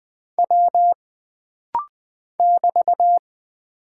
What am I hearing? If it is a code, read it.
Morse code, heard as a keyed tone, about 20 wpm, 700 Hz. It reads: W <BT>
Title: Single Character (Letter:Number:Punctuation:Prosign) - Rapid-Fire - 20wpm